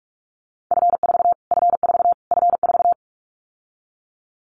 Morse code, heard as a keyed tone, about 44 wpm, 700 Hz. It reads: F4 F4 F4
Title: Callsign Prefixes for Top 25 DXCC - Repeated 3x - 45wpm